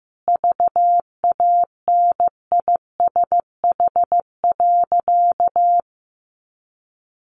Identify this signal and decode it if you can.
Morse code, heard as a keyed tone, about 15 wpm, 700 Hz. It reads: VANISH.
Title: A Sound Of Thunder 15wpm